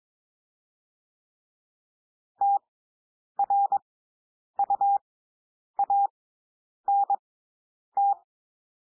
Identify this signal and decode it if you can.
Morse code, heard as a keyed tone, about 22 wpm, spaced as for 9 wpm, 800 Hz. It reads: TRUANT
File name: SOUND WORDS 22 wpm